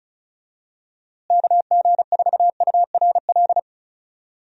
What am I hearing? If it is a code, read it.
Morse code, heard as a keyed tone, about 36 wpm, 700 Hz. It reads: KG4URL
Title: USA Call Signs - 6 Characters - Repeated 2x - 35wpm